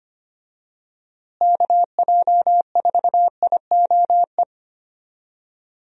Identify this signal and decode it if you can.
Morse code, heard as a keyed tone, about 25 wpm, 700 Hz. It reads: KJ4IOE
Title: USA Call Signs - 6 Characters - Repeated 2x - 25wpm